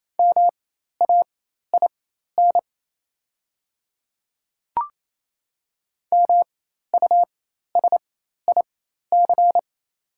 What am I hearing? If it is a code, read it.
Morse code, heard as a keyed tone, about 28 wpm, spaced as for 13 wpm, 700 Hz. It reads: MAIN MUSIC
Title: 4x Character Spacing - Top 300 Words - 28wpm